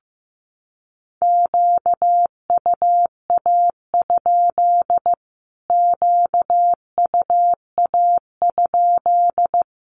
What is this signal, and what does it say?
Morse code, heard as a keyed tone, about 15 wpm, 700 Hz. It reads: QUA? QUA?
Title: Q-Codes Repeated 3x 15wpm